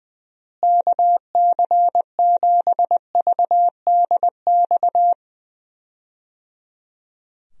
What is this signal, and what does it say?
Morse code, heard as a keyed tone, about 20 wpm, 700 Hz. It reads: KC7VDX
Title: USA Call Signs - 6 Characters - Repeated 3x - 20wpm